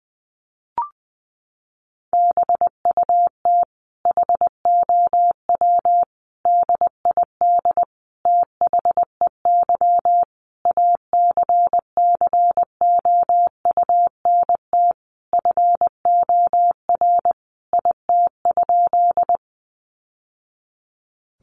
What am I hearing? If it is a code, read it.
Morse code, heard as a keyed tone, about 20 wpm, 700 Hz. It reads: BUT HOW DID THEY ACCOUNT FOR IT?
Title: Sentences from Top 900 Words 20wpm